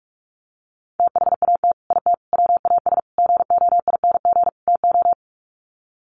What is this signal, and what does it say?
Morse code, heard as a keyed tone, about 44 wpm, 700 Hz. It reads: THAT IT WAS GOING TO
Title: Sets of 5 Words Encore2 45wpm